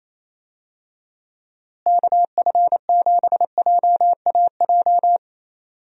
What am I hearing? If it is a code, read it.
Morse code, heard as a keyed tone, about 28 wpm, 700 Hz. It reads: KF7JAJ
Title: USA Call Signs - 6 Characters - Repeated 2x - 28wpm